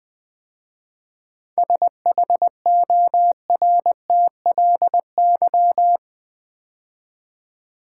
Morse code, heard as a keyed tone, about 20 wpm, 700 Hz. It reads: SHORTLY
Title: 7 Letter Words - First 3 Letters Given - 20wpm